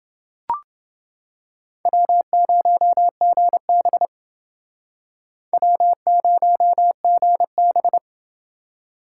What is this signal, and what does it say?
Morse code, heard as a keyed tone, about 28 wpm, 700 Hz. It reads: W0GB W0GB